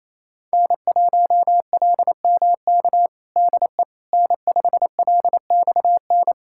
Morse code, heard as a keyed tone, about 28 wpm, 700 Hz. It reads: N1LMK DE N5LXN
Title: USA Callsign DE Callsign - 5 Characters - 28wpm